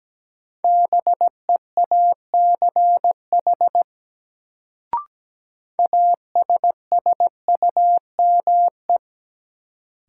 Morse code, heard as a keyed tone, about 17 wpm, 700 Hz. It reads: BEACH ASSUME
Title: Sentences from Top 1500 Words - New Word Drill - 17wpm